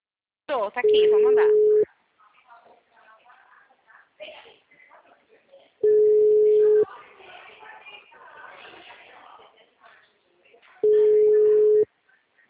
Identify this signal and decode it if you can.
Ringback tone (Europe, 425 Hz), with speech.